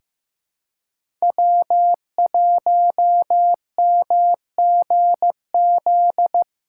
Morse code, heard as a keyed tone, about 15 wpm, 700 Hz. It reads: W1MGZ
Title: USA Callsign DE Callsign - 5 Characters - 15wpm